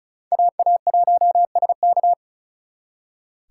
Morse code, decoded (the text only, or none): AA1SK